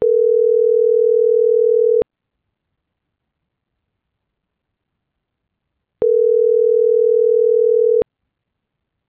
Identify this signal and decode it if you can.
Ringback tone (North America), 440 + 480 Hz.